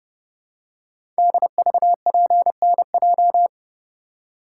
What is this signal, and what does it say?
Morse code, heard as a keyed tone, about 30 wpm, 700 Hz. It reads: DVPNJ
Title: Code Groups - 5 Letters - 30wpm